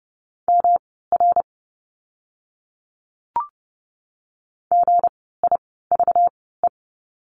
Morse code, heard as a keyed tone, about 30 wpm, spaced as for 17 wpm, 700 Hz. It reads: MR GIVE
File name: course / 30:10wpm / 237 Lesson 38f - Letter q - Words 30wpm